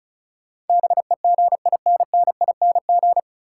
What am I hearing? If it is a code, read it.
Morse code, heard as a keyed tone, about 36 wpm, 700 Hz. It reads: BEGINNING
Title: Sentences from Top 800 Words - New Word Drill - 35wpm